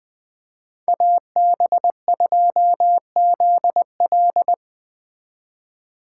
Morse code, heard as a keyed tone, about 20 wpm, 700 Hz. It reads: AB2ZL